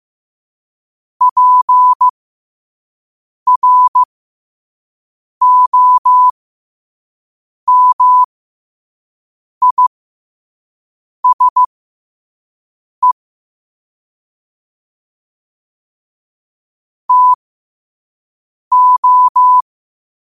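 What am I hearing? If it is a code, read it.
Morse code, heard as a keyed tone, about 15 wpm, spaced as for 5 wpm, 1000 Hz. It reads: PROMISE TO
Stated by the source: Morse at 5 wpm.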